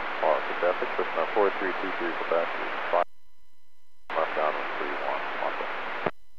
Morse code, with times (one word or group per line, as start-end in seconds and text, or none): none